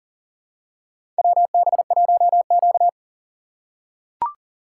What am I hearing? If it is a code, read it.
Morse code, heard as a keyed tone, about 37 wpm, 700 Hz. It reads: WB1Q
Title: USA Call Signs - 6x Character Spacing - 40wpm